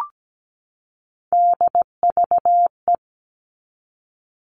Morse code, heard as a keyed tone, about 17 wpm, 700 Hz. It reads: DVE
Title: All Cut Numbers - 3-Digit 17wpm